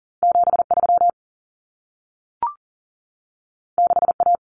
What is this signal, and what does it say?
Morse code, heard as a keyed tone, about 40 wpm, 700 Hz. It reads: 73 6A
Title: Common Cut Numbers - 2-Digit 40wpm